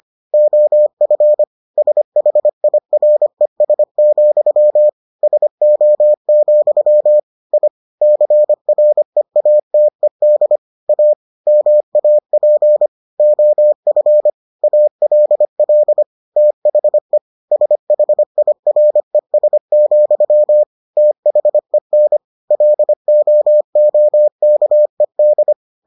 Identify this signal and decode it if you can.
Morse code, heard as a keyed tone, about 25 wpm, 600 Hz. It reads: OF SHIRES, SO, I CREATED A MAP OF ALL THE SHIRES, THEN LOOKED